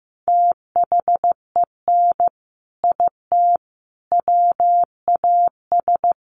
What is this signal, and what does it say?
Morse code, heard as a keyed tone, about 15 wpm, 700 Hz. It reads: THEN IT WAS
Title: Sentences from Top 1300 Words 15wpm